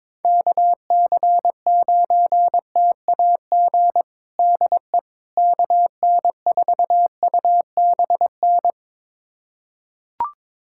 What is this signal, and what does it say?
Morse code, heard as a keyed tone, about 22 wpm, 700 Hz. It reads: KC9TAG DE KN4UBN
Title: USA Callsign DE Callsign - 6 Characters - 22wpm